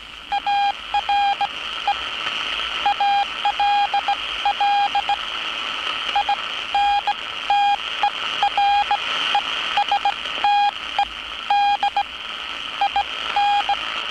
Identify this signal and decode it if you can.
Morse code, heard as a keyed tone, about 17 wpm, spaced as for 14 wpm, 810 Hz, with radio noise. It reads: ARE ALL INTERESTED IN
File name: training / NAVPERS 11049to11060 / NAVPERS 11059 RA RB